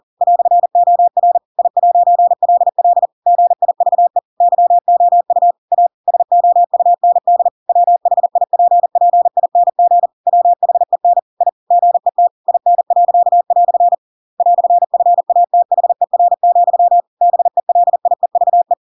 Morse code, heard as a keyed tone, about 40 wpm, 700 Hz. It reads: "OR I'LL GIVE YOU A SOUND WHIPPING WHEN I GET IN." "FATHER, BELIEVE
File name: Pinocchio / Pinocchio ch7 40 40 ik1hge